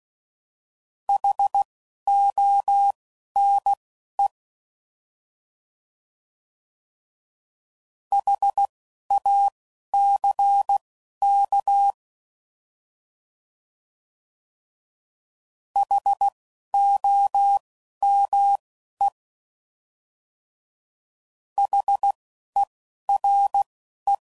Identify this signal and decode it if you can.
Morse code, heard as a keyed tone, about 16 wpm, spaced as for 11 wpm, 800 Hz. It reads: HONE HACK HOME HERE